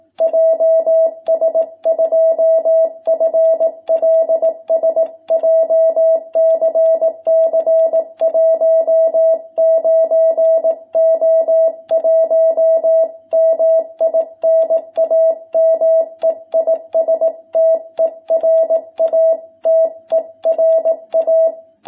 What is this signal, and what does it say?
Morse code, heard as a keyed tone, about 18 wpm, 660 Hz. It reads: JS2FLS JCC 19O1 MI NAME IS TERA TERA
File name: jh8cbh / 2021cw